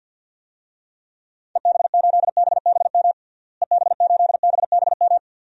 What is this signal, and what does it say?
Morse code, heard as a keyed tone, about 53 wpm, 700 Hz. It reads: EB8BBM EB8BBM
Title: DXCC-100 Call Signs - 6 Characters - Repeated 2x - 50wpm